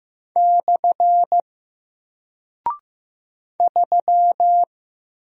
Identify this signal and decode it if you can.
Morse code, heard as a keyed tone, about 15 wpm, 700 Hz. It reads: / 3
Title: Single Character (Letter:Number:Punctuation:Prosign) Alt 15wpm